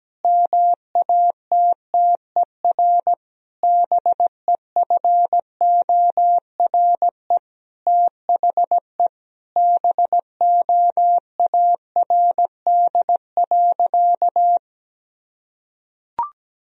Morse code, heard as a keyed tone, about 17 wpm, 700 Hz. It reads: MATTER BEFORE THE BOARD.
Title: Sentences from Top 1000 Words - Review Part 4 - 17wpm